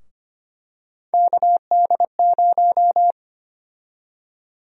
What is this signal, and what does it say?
Morse code, heard as a keyed tone, about 25 wpm, 700 Hz. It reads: KD0
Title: USA Call Sign Prefixes - 25wpm